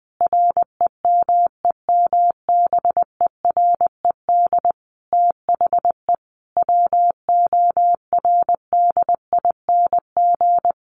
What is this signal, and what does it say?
Morse code, heard as a keyed tone, about 20 wpm, 700 Hz. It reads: REMEMBERED THE WORDING
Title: A Sound Of Thunder 20wpm